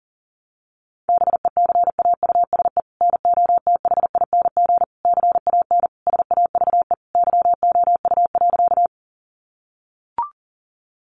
Morse code, heard as a keyed tone, about 40 wpm, 700 Hz. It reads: BECAUSE NOTHING CAN SAVE YOU.
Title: Sentences from Top 1100 Words 40wpm